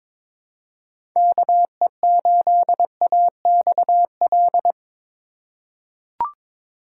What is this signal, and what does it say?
Morse code, heard as a keyed tone, about 22 wpm, 700 Hz. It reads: KE8AXL
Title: USA Call Signs - 6 Characters - 4x Character Spacing - 22wpm